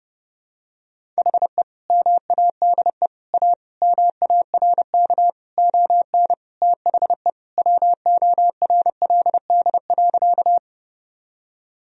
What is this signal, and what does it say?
Morse code, heard as a keyed tone, about 30 wpm, 700 Hz. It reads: HE MADE A MARK ON THE WORLD.
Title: Sentences from Top 1000 Words - Review Part 1 - 30wpm